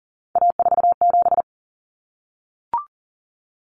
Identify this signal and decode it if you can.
Morse code, heard as a keyed tone, about 40 wpm, 700 Hz. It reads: A47
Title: Common Cut Numbers - 3-Digit 40wpm